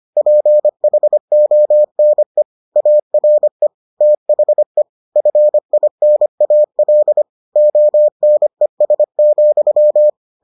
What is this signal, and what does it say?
Morse code, heard as a keyed tone, about 25 wpm, 600 Hz. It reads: PHONE ARE THE FINAL ONES,